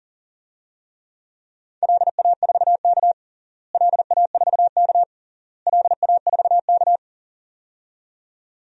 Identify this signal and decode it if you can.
Morse code, heard as a keyed tone, about 40 wpm, 700 Hz. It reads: LA4K LA4K LA4K
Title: DXCC-100 Call Signs - 4 Characters - Repeated 3x - 40wpm